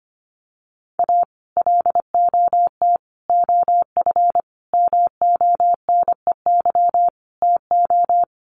Morse code, heard as a keyed tone, about 25 wpm, 700 Hz. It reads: A LOT OF MONEY TO
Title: Sets of 5 Words Encore2 25wpm